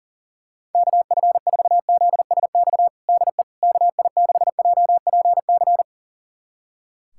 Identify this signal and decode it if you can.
Morse code, heard as a keyed tone, about 40 wpm, 700 Hz. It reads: KF4ZSX DE KI6JPC
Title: USA Callsign DE Callsign 40wpm